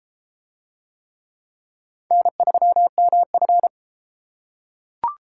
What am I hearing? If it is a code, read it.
Morse code, heard as a keyed tone, about 30 wpm, 700 Hz. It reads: N3MF